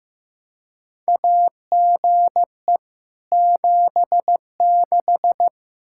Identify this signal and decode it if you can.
Morse code, heard as a keyed tone, about 15 wpm, 700 Hz. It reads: AGE 76
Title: Practice QSO - Exchange 3 - More Info - 15wpm